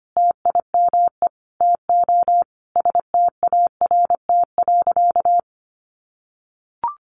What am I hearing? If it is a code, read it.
Morse code, heard as a keyed tone, about 25 wpm, 700 Hz. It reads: TIME TO START.
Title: Sentences from Top 300 Words 25wpm